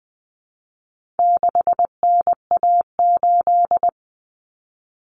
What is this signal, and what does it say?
Morse code, heard as a keyed tone, about 20 wpm, 700 Hz. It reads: 6NA8
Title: Common Cut Numbers - 4-Digit 20wpm